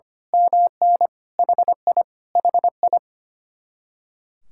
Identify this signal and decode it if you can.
Morse code, heard as a keyed tone, about 24 wpm, 700 Hz. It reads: MN HI HI
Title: Speed-Racing Common QSO Phrases - 25wpm